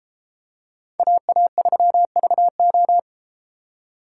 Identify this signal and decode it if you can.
Morse code, heard as a keyed tone, about 33 wpm, 700 Hz. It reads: AA3VO